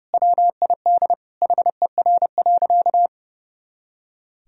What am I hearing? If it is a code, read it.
Morse code, heard as a keyed tone, about 30 wpm, 700 Hz. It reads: WID HER.
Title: Sentences from Top 100 Words and Abbreviations 30wpm